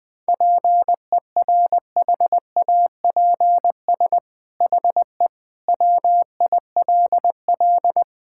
Morse code, heard as a keyed tone, about 21 wpm, 700 Hz. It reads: PERHAPS HE WILL
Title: Sentences from Top 1600 Words 20wpm